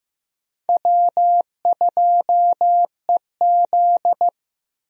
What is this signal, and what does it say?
Morse code, heard as a keyed tone, about 15 wpm, 700 Hz. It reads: W2EZ
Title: USA Call Signs - 4 Characters - Repeated 3x - 15wpm